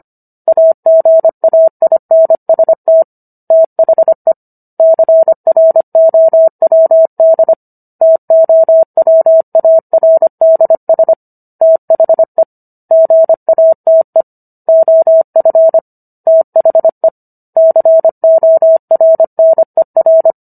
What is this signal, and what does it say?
Morse code, heard as a keyed tone, about 25 wpm, 650 Hz. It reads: AGAINST THE CROWD TOWARDS THE GATE OF THE CORNER